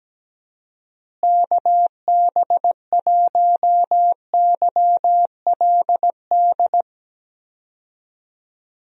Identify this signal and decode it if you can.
Morse code, heard as a keyed tone, about 17 wpm, 700 Hz. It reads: KB1YLD
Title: USA Call Signs - 6 Characters - Repeated 2x - 17wpm